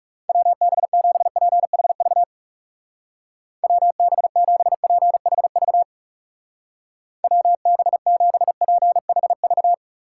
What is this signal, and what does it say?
Morse code, heard as a keyed tone, about 40 wpm, 700 Hz. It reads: WB7PHV WB7PHV WB7PHV